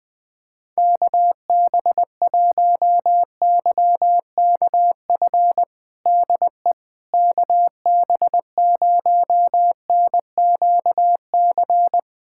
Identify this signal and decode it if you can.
Morse code, heard as a keyed tone, about 20 wpm, 700 Hz. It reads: KB1YKF DE KB0NQC
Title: USA Callsign DE Callsign - 6 Characters - 20wpm